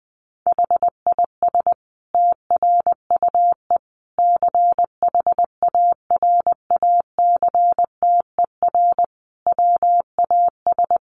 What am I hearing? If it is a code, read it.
Morse code, heard as a keyed tone, about 20 wpm, 700 Hz. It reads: HIS TRUE CHARACTER WAS
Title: Sentences from Top 1200 Words 20wpm